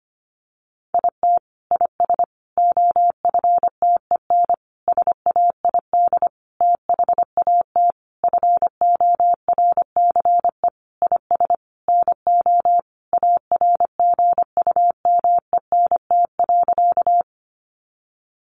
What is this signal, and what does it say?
Morse code, heard as a keyed tone, about 25 wpm, 700 Hz. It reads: IT IS OFTEN SAID THAT FORCE IS NO ARGUMENT.
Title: Sentences from Top 1400 Words 25wpm